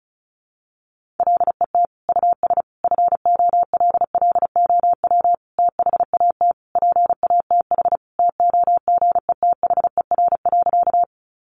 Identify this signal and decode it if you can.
Morse code, heard as a keyed tone, about 34 wpm, 700 Hz. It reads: LET US FOLLOW THAT PATH TOGETHER.
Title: Sentences from Top 1400 Words 35wpm